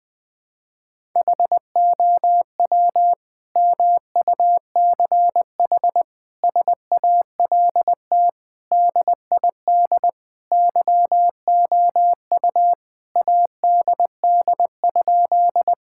Morse code, heard as a keyed tone, about 21 wpm, 700 Hz. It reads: HOW MUCH SALT DID YOU ADD?
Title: Sentences from Top 1000 Words - Review Part 5 - 20wpm